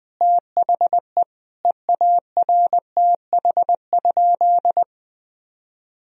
Morse code, heard as a keyed tone, about 20 wpm, 700 Hz. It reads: THE EARTH?
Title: Sentences from Top 1000 Words - Review Part 3 - 20wpm